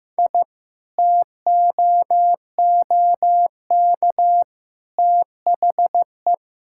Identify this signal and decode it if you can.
Morse code, heard as a keyed tone, about 15 wpm, 700 Hz. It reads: I TOOK THE
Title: Sentences from Top 1000 Words - Review Part 5 - 15wpm